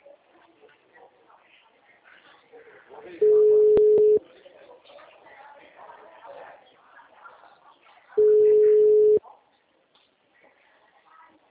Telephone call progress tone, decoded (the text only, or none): ringback tone